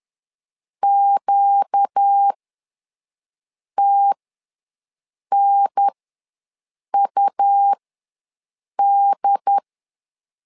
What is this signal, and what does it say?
Morse code, heard as a keyed tone, about 10 wpm, spaced as for 5 wpm, 800 Hz. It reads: QTNUD
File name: LEZIONI / 4 Q ws 10wpm 2306